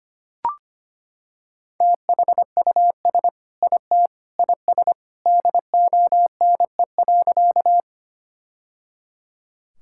Morse code, heard as a keyed tone, about 25 wpm, 700 Hz. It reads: THUS IT IS DONE.
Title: Sentences from Top 1000 Words - Review Part 5 - 25wpm